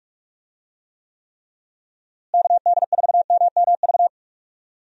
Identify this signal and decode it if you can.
Morse code, heard as a keyed tone, about 42 wpm, 700 Hz. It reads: KD4MMV
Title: USA Call Signs - 6 Characters - Repeated 3x - 45wpm